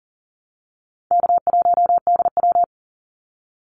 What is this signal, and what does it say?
Morse code, heard as a keyed tone, about 40 wpm, 700 Hz. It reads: K1DW
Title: Easy Call Signs - 40wpm